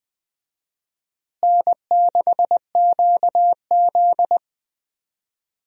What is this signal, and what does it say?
Morse code, heard as a keyed tone, about 20 wpm, 700 Hz. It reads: N6QZ
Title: USA Call Signs - 4 Characters - Repeated 2x - 20wpm